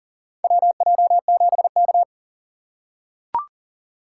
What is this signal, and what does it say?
Morse code, heard as a keyed tone, about 39 wpm, 700 Hz. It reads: WJ7K
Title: USA Call Signs - 3x Character Spacing - 40wpm